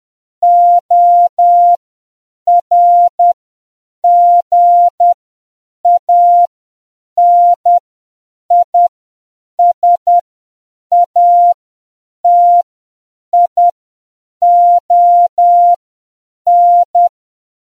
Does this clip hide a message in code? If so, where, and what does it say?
Yes: Morse code from 0.4 to 17.1 s, heard as a keyed tone, about 10 wpm, spaced as for 7 wpm, 700 Hz. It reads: ORGANISATION